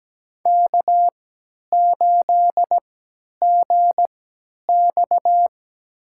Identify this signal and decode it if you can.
Morse code, heard as a keyed tone, about 17 wpm, spaced as for 10 wpm, 700 Hz. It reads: K8GX
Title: USA Call Signs - 3x Character Spacing - 17wpm